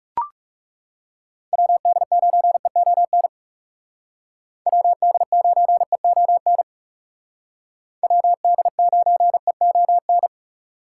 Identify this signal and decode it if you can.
Morse code, heard as a keyed tone, about 37 wpm, 700 Hz. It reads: WD9EON WD9EON WD9EON
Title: Speed Racing - USA Call Signs 30wpm